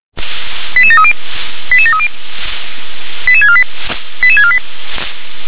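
Five-tone selective call (ZVEI), heard as quick runs of five tones, tones of about 70 ms. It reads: 88730 88730 88648 88648
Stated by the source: ZVEI five-tone selective call 88150 received by SDR.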